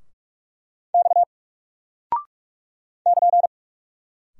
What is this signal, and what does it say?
Morse code, heard as a keyed tone, about 44 wpm, 700 Hz. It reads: X <KN>
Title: Single Character (Letter:Number:Punctuation:Prosign) 45wpm